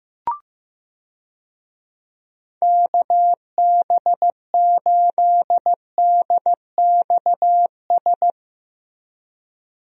Morse code, heard as a keyed tone, about 15 wpm, 700 Hz. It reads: KB8DXS
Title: USA Call Signs - 6 Characters - Repeated 2x - 15wpm